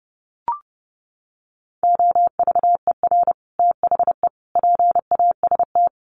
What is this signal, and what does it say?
Morse code, heard as a keyed tone, about 30 wpm, 700 Hz. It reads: OVER THE PAST
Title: Sets of 3 Words 30wpm